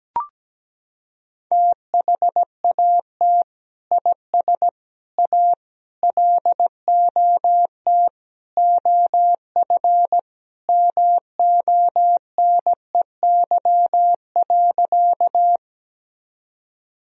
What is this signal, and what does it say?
Morse code, heard as a keyed tone, about 17 wpm, 700 Hz. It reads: THAT IS A LOT OF MONEY.
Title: Sentences from Top 1000 Words - Review Part 3 - 17wpm